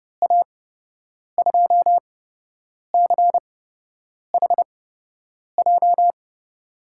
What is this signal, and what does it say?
Morse code, heard as a keyed tone, about 30 wpm, spaced as for 8 wpm, 700 Hz. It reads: A2CHJ
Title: USA Call Signs - 6 Characters - 8x Character Spacing - 30wpm